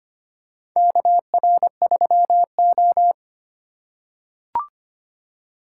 Morse code, heard as a keyed tone, about 25 wpm, 700 Hz. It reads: KR3O
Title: USA Call Signs - 4 Characters - 3x Character Spacing - 25wpm